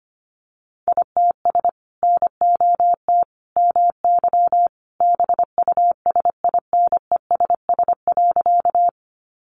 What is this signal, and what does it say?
Morse code, heard as a keyed tone, about 25 wpm, 700 Hz. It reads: ITS NOT MY BUSINESS.
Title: Sentences from Top 600 Words 25wpm